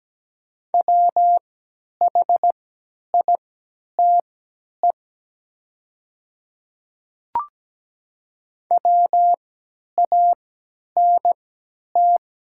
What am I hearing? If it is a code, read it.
Morse code, heard as a keyed tone, about 17 wpm, spaced as for 10 wpm, 700 Hz. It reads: WHITE WANT
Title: 3x Character Spacing - Top 300 Words - 17wpm